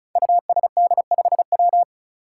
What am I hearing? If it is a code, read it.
Morse code, heard as a keyed tone, about 36 wpm, 700 Hz. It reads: USD5W
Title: Five Challenging Letter-Number Combinations 35wpm